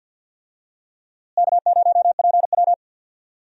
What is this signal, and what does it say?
Morse code, heard as a keyed tone, about 48 wpm, 700 Hz. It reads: K0PW
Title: USA Call Signs - 4 Characters - Repeated 3x - 50wpm